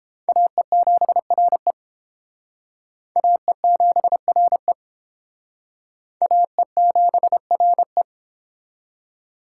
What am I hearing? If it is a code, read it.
Morse code, heard as a keyed tone, about 31 wpm, 700 Hz. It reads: AE7RE AE7RE AE7RE